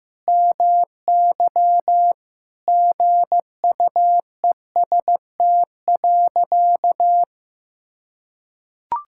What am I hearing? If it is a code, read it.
Morse code, heard as a keyed tone, about 15 wpm, 700 Hz. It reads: MY GUEST.
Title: Sentences from Top 1600 Words 15wpm